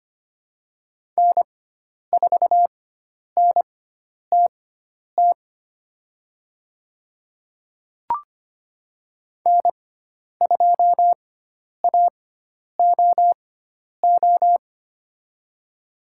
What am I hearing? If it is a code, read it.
Morse code, heard as a keyed tone, about 25 wpm, spaced as for 10 wpm, 700 Hz. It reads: N4NTT N2AOO
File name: Lesson 16g - Letter c - Call Signs 25wpm